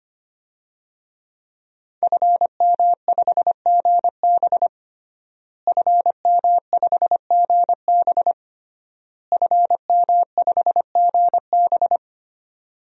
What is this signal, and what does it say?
Morse code, heard as a keyed tone, about 25 wpm, 700 Hz. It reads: FM5GB FM5GB FM5GB
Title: DXCC-100 Call Signs - 5 Characters - Repeated 3x - 25wpm